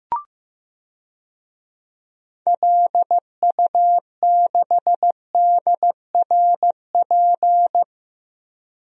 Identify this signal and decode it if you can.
Morse code, heard as a keyed tone, about 15 wpm, 700 Hz. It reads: LU6DRP
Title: DXCC-100 Call Signs - 6 Characters - Repeated 3x - 15wpm